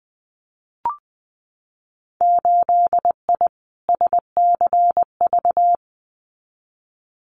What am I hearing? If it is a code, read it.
Morse code, heard as a keyed tone, about 20 wpm, 700 Hz. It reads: 8I SCV
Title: Winter Field Day Exchanges Encore 20wpm